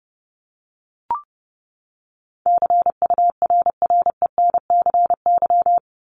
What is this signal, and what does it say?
Morse code, heard as a keyed tone, about 30 wpm, 700 Hz. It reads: CURRENCY
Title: Top 1251-1500 Words 30wpm